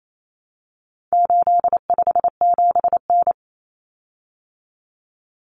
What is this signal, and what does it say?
Morse code, heard as a keyed tone, about 28 wpm, 700 Hz. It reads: 857N
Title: Common Cut Numbers - 4-Digit 28wpm